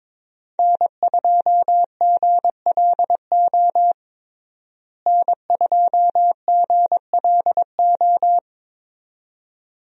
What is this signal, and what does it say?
Morse code, heard as a keyed tone, about 22 wpm, 700 Hz. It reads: N2GLO N2GLO
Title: USA Call Signs - 5 Characters - Repeated 2x - 22wpm